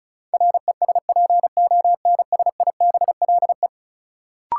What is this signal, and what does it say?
Morse code, heard as a keyed tone, about 36 wpm, 700 Hz. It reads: RESPONSIBLE
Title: Sentences from Top 1000 Words - Review Part 7 - New Word Drill - 35wpm